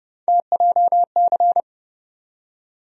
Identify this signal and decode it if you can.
Morse code, heard as a keyed tone, about 29 wpm, 700 Hz. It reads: TJC
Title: USA Call Sign Postfixes - 3 Letters 30wpm